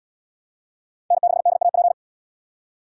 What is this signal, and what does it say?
Morse code, heard as a keyed tone, about 71 wpm, 700 Hz. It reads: N6DAP